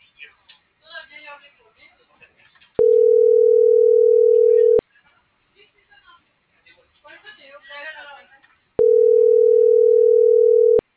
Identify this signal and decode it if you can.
Ringback tone (North America), 440 + 480 Hz, with speech.
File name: CENTRAL MABE